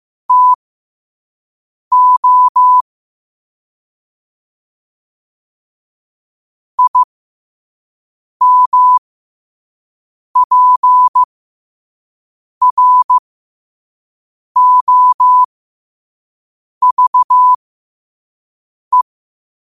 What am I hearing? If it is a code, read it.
Morse code, heard as a keyed tone, about 15 wpm, spaced as for 5 wpm, 1000 Hz. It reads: TO IMPROVE
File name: qotdfiles / 5wpm